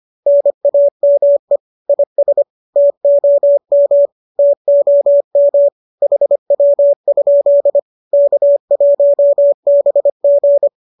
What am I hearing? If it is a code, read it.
Morse code, heard as a keyed tone, about 25 wpm, 570 Hz. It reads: NAME IS TOM TOM HW? K1BG